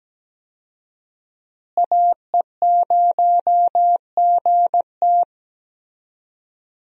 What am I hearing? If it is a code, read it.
Morse code, heard as a keyed tone, about 17 wpm, 700 Hz. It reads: AE0GT
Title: USA Call Signs - 5 Characters - Repeated 2x - 17wpm